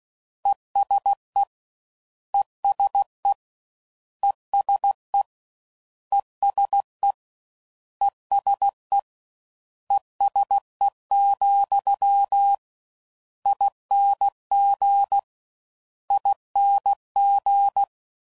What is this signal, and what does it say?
Morse code, heard as a keyed tone, about 16 wpm, 800 Hz. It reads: ESE ESE ESE ESE ESE ESE, ING ING
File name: Suffixes / General / ESE INGs